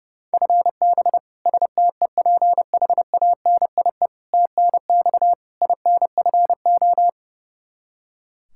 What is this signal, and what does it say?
Morse code, heard as a keyed tone, about 30 wpm, 700 Hz. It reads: FB STEPHANIE TNX INFO
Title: Practice QSO - Exchange 3 - More Info - 30wpm